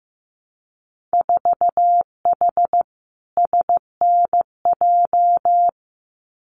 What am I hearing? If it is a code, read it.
Morse code, heard as a keyed tone, about 15 wpm, 700 Hz. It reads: 4H SNJ
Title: Winter Field Day Exchanges Encore 15wpm